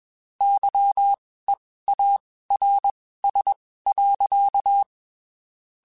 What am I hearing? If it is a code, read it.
Morse code, heard as a keyed tone, about 21 wpm, spaced as for 15 wpm, 800 Hz. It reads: YEARS.
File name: (spaced) / QSOs Spaced-mp3s / WA7QOX 2 @20 sp